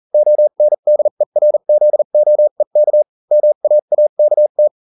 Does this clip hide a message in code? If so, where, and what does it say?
Yes: Morse code from 0.1 to 4.7 s, heard as a keyed tone, about 40 wpm, spaced as for 35 wpm, 600 Hz. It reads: ONDERZOEK MAAKT